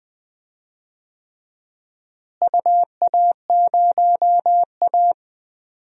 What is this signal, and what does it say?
Morse code, heard as a keyed tone, about 20 wpm, 700 Hz. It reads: UA0A
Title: DXCC-100 Call Signs - 4 Characters - Repeated 3x - 20wpm